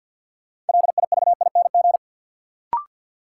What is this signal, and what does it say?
Morse code, heard as a keyed tone, about 47 wpm, 700 Hz. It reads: LIVING
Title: Sentences from Top 1000 Words - Review Part 6 - New Word Drill - 50wpm